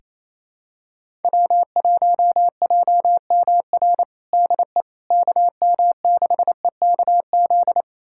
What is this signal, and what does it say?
Morse code, heard as a keyed tone, about 28 wpm, 700 Hz. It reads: W1JMR DE KM6EKZ
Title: USA Callsign DE Callsign 28wpm